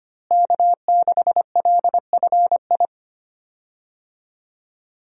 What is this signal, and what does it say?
Morse code, heard as a keyed tone, about 25 wpm, 700 Hz. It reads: K6LFI
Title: USA Call Signs - 5 Characters - Repeated 3x - 25wpm